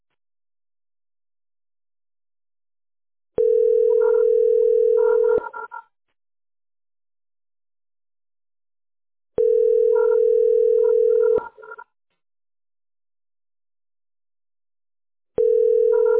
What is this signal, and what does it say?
Ringback tone (North America), 440 + 480 Hz.
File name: e78029de-100e-4318-9a57